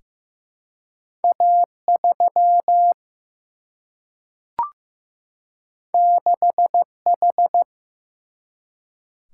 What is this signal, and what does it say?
Morse code, heard as a keyed tone, about 15 wpm, 700 Hz. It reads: A3 6H
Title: Single-Letter Single-Number 15wpm